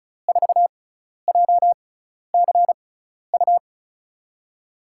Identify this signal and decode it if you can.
Morse code, heard as a keyed tone, about 36 wpm, spaced as for 12 wpm, 700 Hz. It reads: 4JCU
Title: USA Call Signs - 5 Characters - 6x Character Spacing - 35wpm